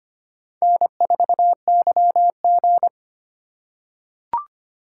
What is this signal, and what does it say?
Morse code, heard as a keyed tone, about 25 wpm, 700 Hz. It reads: N4YG
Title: USA Call Signs - 4 Characters - 3x Character Spacing - 25wpm